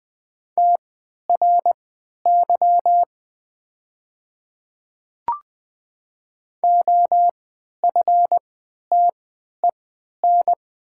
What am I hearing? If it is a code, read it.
Morse code, heard as a keyed tone, about 20 wpm, spaced as for 11 wpm, 700 Hz. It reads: TRY OFTEN
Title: 3x Character Spacing - Top 300 Words - 20wpm